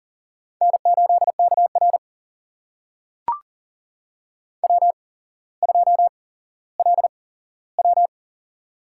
Morse code, heard as a keyed tone, about 40 wpm, spaced as for 17 wpm, 700 Hz. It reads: N8KR W2LW
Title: USA Call Signs - 4 Characters - 8x Character Spacing - 40wpm